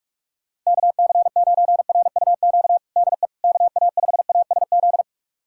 Morse code, heard as a keyed tone, about 46 wpm, 700 Hz. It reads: KC9RUQ DE KA5ASZ
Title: USA Callsign DE Callsign - 6 Characters - 45wpm